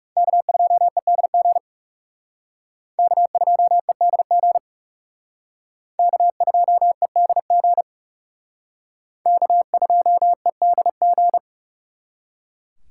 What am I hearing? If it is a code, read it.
Morse code, heard as a keyed tone, about 37 wpm, 700 Hz. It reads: K2EDG K2EDG K2EDG K2EDG
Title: Speed Racing - USA 1x3 Call Signs 30wpm